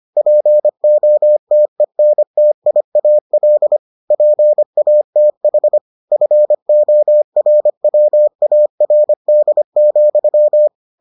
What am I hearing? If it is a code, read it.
Morse code, heard as a keyed tone, about 25 wpm, 600 Hz. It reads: POTENTIAL PATH FORWARD,